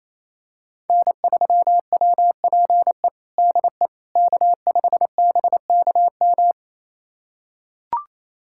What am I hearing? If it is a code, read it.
Morse code, heard as a keyed tone, about 28 wpm, 700 Hz. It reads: N3WPE DE K5BKM
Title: USA Callsign DE Callsign - 5 Characters - 28wpm